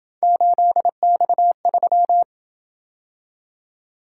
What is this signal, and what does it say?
Morse code, heard as a keyed tone, about 27 wpm, 700 Hz. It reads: 8X3